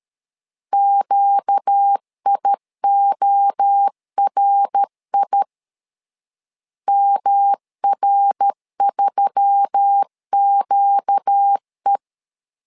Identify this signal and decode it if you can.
Morse code, heard as a keyed tone, about 13 wpm, 800 Hz. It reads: QIORI MR3QE
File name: LEZIONI / 4 Q1S3 12wpm 2023